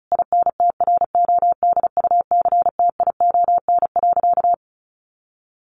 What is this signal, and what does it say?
Morse code, heard as a keyed tone, about 34 wpm, 700 Hz. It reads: INTRODUCTION.
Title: Sentences from Top 1000 Words 35wpm